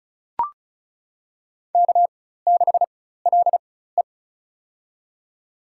Morse code, heard as a keyed tone, about 36 wpm, spaced as for 16 wpm, 700 Hz. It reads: K6LE
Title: USA Call Signs - 4 Characters - 4x Character Spacing - 35wpm